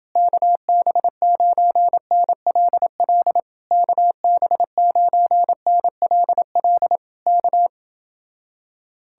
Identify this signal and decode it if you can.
Morse code, heard as a keyed tone, about 30 wpm, 700 Hz. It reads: KB9NLL KB9NLL K